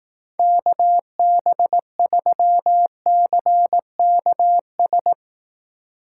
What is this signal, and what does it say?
Morse code, heard as a keyed tone, about 18 wpm, 700 Hz. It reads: KB3CKS
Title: Speed Racing - USA 2x3 Call Signs 15wpm